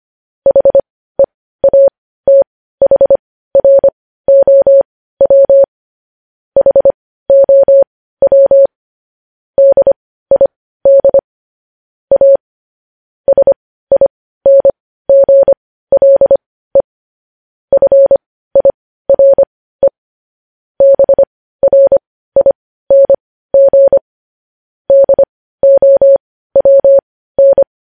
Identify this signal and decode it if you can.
Morse code, heard as a keyed tone, about 25 wpm, spaced as for 15 wpm, 550 Hz. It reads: HEATHROW HOW DID A SINGLE FIRE BRING DOWN
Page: News Headlines in Morse Code at 15 WPM